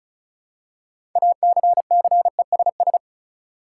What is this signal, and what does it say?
Morse code, heard as a keyed tone, about 36 wpm, 700 Hz. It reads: ACCESS
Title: Six Letter Words - 35wpm